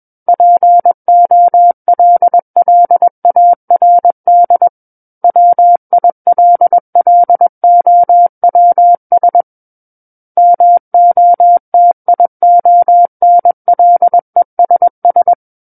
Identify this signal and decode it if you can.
Morse code, heard as a keyed tone, about 21 wpm, 700 Hz. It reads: POLLARD WILLOWS MOTIONLESS
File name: Book One / 12-Chapter 12 - 21 WPM